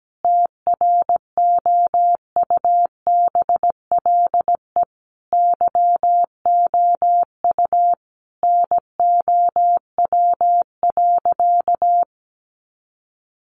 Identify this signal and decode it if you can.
Morse code, heard as a keyed tone, about 17 wpm, 700 Hz. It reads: TROUBLE YOU NOW.
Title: Sentences from Top 1200 Words 17wpm